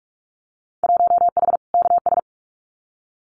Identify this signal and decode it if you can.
Morse code, heard as a keyed tone, about 44 wpm, 700 Hz. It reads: 1H KS
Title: Winter Field Day Exchanges 45wpm